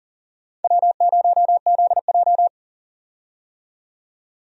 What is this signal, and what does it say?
Morse code, heard as a keyed tone, about 40 wpm, 700 Hz. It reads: W0ZJ
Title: USA Call Signs - 4 Characters - 40wpm